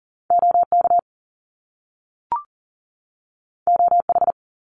Morse code, heard as a keyed tone, about 40 wpm, 700 Hz. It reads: OK OH
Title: Top Two Letter Combinations in Words 40wpm